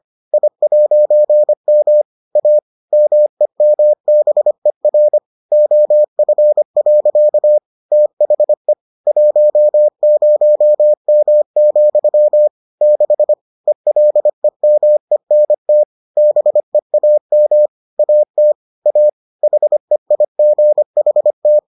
Morse code, heard as a keyed tone, about 25 wpm, 600 Hz. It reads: I'M A MEMBER OF. THE 10M, 6 ELEMENT BEAM AT A HEIGHT